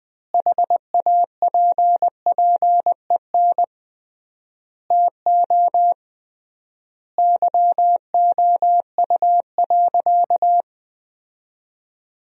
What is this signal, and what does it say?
Morse code, heard as a keyed tone, about 20 wpm, 700 Hz. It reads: HAPPEN TO YOU.
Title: 3x Word-Spacing - Sentences from Top 500 Words - 20wpm